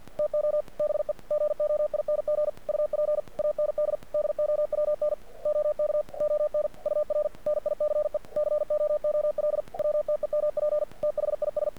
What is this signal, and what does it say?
Morse code, heard as a keyed tone, about 48 wpm, 600 Hz. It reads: TO BE GOING UP AND DOWN OK ON UR NICE COOL WTEMP THIS
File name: archived / CRSnet